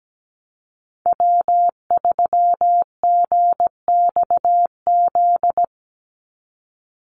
Morse code, heard as a keyed tone, about 17 wpm, 700 Hz. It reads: W3GXZ
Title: USA Call Signs 17wpm